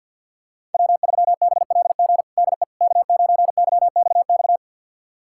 Morse code, heard as a keyed tone, about 52 wpm, 700 Hz. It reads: W3BLG DE K9YXX
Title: USA Callsign DE Callsign - 5 Characters - 50wpm